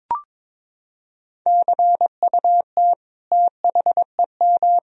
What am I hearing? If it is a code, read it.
Morse code, heard as a keyed tone, about 22 wpm, 700 Hz. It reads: CUT THEM
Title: Sentences from Top 1000 Words - Review Part 1 - 22wpm